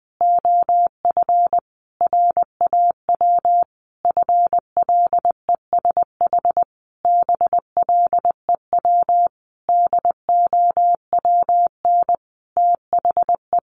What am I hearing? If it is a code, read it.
Morse code, heard as a keyed tone, about 20 wpm, 700 Hz. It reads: OF RAW FLESH BLEW DOWN THE
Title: A Sound Of Thunder 20wpm